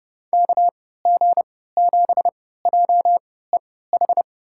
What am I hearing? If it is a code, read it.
Morse code, heard as a keyed tone, about 30 wpm, spaced as for 17 wpm, 700 Hz. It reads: KG7JEH
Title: USA Call Signs - 6 Characters - 3x Character Spacing - 30wpm